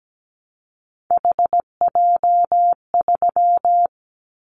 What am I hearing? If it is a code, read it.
Morse code, heard as a keyed tone, about 17 wpm, 700 Hz. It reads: HJ3
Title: Callsign Prefixes for Top 100 DXCC - 17wpm